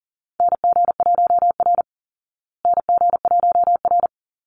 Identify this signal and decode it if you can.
Morse code, heard as a keyed tone, about 40 wpm, 700 Hz. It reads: NG1R NG1R
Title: Call Signs Repeated 2x - 40wpm